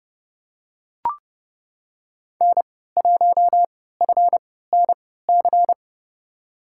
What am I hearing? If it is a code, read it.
Morse code, heard as a keyed tone, about 30 wpm, spaced as for 17 wpm, 700 Hz. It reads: N1FNC
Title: USA Call Signs - 5 Characters - 3x Character Spacing - 30wpm